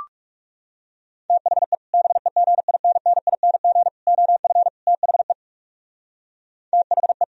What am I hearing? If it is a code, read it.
Morse code, heard as a keyed tone, about 36 wpm, 700 Hz. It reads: THE BEGINNING OF THE THE
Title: Speed-Racing - Sets of 4 Words 30wpm